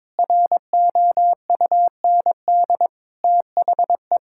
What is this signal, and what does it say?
Morse code, heard as a keyed tone, about 21 wpm, 700 Hz. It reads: ROUND THE
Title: Sentences from Top 1600 Words 22wpm